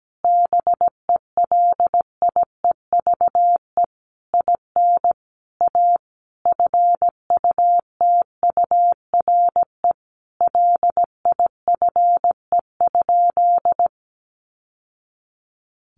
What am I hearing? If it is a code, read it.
Morse code, heard as a keyed tone, about 17 wpm, 700 Hz. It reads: BELIEVE IN A FUTURE LIFE?
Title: Sentences from Top 1000 Words 17wpm